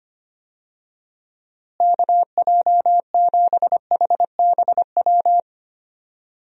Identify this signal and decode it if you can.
Morse code, heard as a keyed tone, about 25 wpm, 700 Hz. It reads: KJ7HBW
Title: USA Call Signs - 6 Characters - Repeated 3x - 25wpm